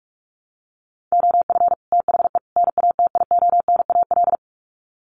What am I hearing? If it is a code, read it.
Morse code, heard as a keyed tone, about 44 wpm, 700 Hz. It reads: OF THE NATIONAL
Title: Sets of 3 Words 45wpm